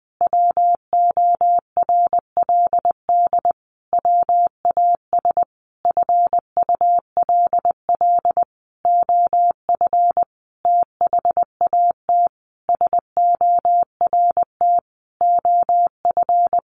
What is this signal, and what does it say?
Morse code, heard as a keyed tone, about 20 wpm, 700 Hz. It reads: WORLD WAS FULL OF THAT SORT OF
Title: Sentences from Top 1300 Words 20wpm